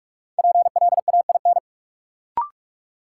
Morse code, heard as a keyed tone, about 41 wpm, 700 Hz. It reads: PLAIN
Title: Sentences from Top 400 Words - New Word Drill - 45wpm